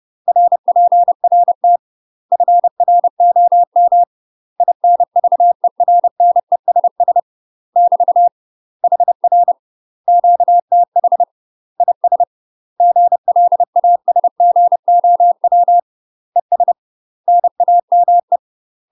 Morse code, heard as a keyed tone, about 30 wpm, 700 Hz. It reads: RPRT FROM INVERNESS <BT> HR QTH IS GLASGOW ES NAME
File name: UK MASTER QSOs TRAINING / 30WPM / UK MASTER QSOs 0038